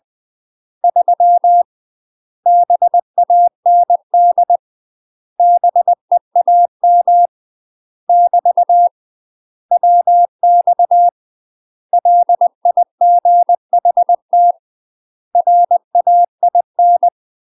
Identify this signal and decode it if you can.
Morse code, heard as a keyed tone, about 20 wpm, 700 Hz. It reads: 3 BAND BEAM <BT> WX LIGHT RAIN
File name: UK MASTER QSOs TRAINING / 20WPM / UK MASTER QSOs 0043